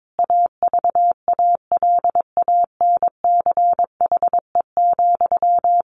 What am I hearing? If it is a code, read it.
Morse code, heard as a keyed tone, about 22 wpm, 700 Hz. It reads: AVALANCHE,
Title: A Sound Of Thunder 22wpm